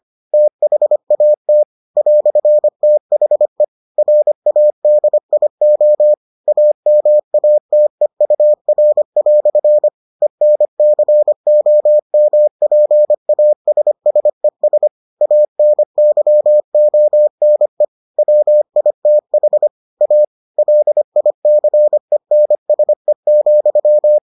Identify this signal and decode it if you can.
Morse code, heard as a keyed tone, about 25 wpm, 600 Hz. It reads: THAT "THE RADIO AMATEUR" ENCOMPASSES ANYONE WITH A LICENSE,